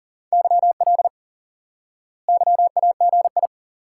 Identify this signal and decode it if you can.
Morse code, heard as a keyed tone, about 40 wpm, 700 Hz. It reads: YL YAGI
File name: IQE Course 3 - Lesson 89 - YAGI - Two Elements - 40wpm